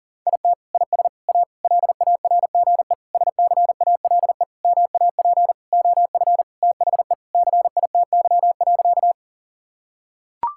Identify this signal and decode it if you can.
Morse code, heard as a keyed tone, about 40 wpm, 700 Hz. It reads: IT IS A LARGE SCALE MAP OF THE CITY.
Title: Sentences from Top 1000 Words - Review Part 5 - 40wpm